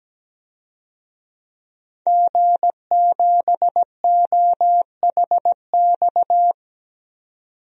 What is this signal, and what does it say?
Morse code, heard as a keyed tone, about 17 wpm, 700 Hz. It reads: G7OHX
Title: DXCC-100 Call Signs - 5 Characters - 17wpm